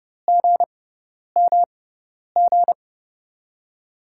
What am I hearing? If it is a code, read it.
Morse code, heard as a keyed tone, about 31 wpm, spaced as for 10 wpm, 700 Hz. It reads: GMG
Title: USA Call Signs - 6x Character Spacing - 30wpm